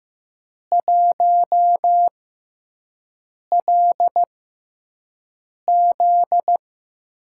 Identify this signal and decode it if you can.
Morse code, heard as a keyed tone, about 15 wpm, spaced as for 5 wpm, 700 Hz. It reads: 1LZ
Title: USA Call Signs - 5 Characters - 6x Character Spacing - 15wpm